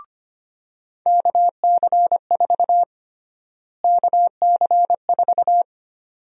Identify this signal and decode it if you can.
Morse code, heard as a keyed tone, about 25 wpm, 700 Hz. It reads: KC4 KC4
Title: USA Call Sign Prefixes - Repeated 2x - 25wpm